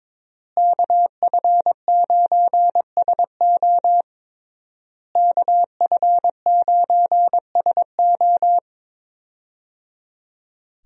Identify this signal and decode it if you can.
Morse code, heard as a keyed tone, about 22 wpm, 700 Hz. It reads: KF9SO KF9SO
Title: USA Call Signs - 5 Characters - Repeated 3x - 22wpm